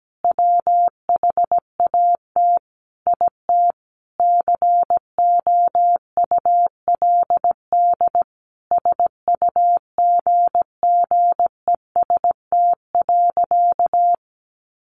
Morse code, heard as a keyed tone, about 17 wpm, 700 Hz. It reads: WHAT IT COULD SUGGEST.
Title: Sentences from Top 1500 Words 17wpm